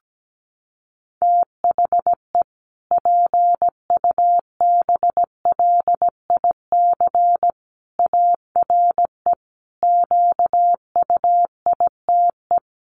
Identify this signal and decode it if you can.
Morse code, heard as a keyed tone, about 17 wpm, 700 Hz. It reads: THE PUBLIC ARE QUITE
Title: Sentences from Top 1300 Words 17wpm